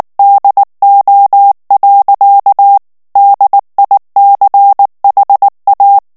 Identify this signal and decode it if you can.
Morse code, heard as a keyed tone, about 19 wpm, 800 Hz. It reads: DO. DICHA
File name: morse mp3 / ESPAGNOL TEXTE EN CLAIR 2 VITESSE 18 MOTS MINUTES